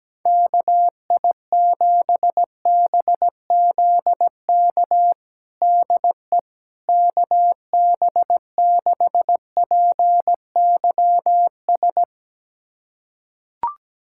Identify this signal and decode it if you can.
Morse code, heard as a keyed tone, about 17 wpm, 700 Hz. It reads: KI7BZK DE KB6PYS
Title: USA Callsign DE Callsign - 6 Characters - 17wpm